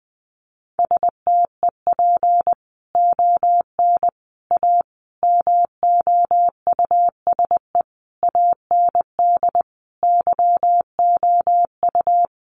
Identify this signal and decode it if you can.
Morse code, heard as a keyed tone, about 20 wpm, 700 Hz. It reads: STEP ON A MOUSE AND YOU
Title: A Sound Of Thunder 20wpm